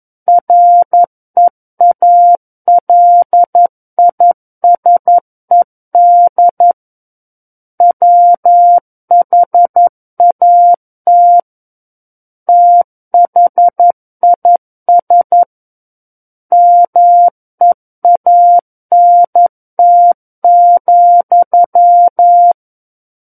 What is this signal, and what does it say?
Morse code, heard as a keyed tone, about 11 wpm, 700 Hz. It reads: REALISED WHAT THIS MEANT,
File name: 11 WPM